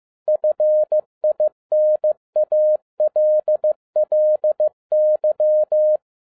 Morse code, heard as a keyed tone, about 15 wpm, 610 Hz. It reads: FINALLY
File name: FIELD DAY CONTEST QRV FROM SAKAE VILLAGE 15wpm